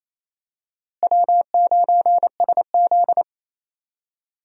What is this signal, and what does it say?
Morse code, heard as a keyed tone, about 28 wpm, 700 Hz. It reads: W9SZ
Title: USA Call Signs - 4 Characters - Repeated 3x - 28wpm